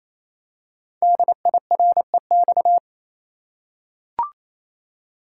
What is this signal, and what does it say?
Morse code, heard as a keyed tone, about 27 wpm, 700 Hz. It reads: DIREX
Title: Code Groups - 5 Letters - 28wpm